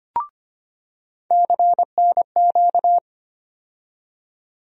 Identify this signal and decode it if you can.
Morse code, heard as a keyed tone, about 25 wpm, 700 Hz. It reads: CNQ
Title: USA Call Sign Postfixes - 3 Letters 25wpm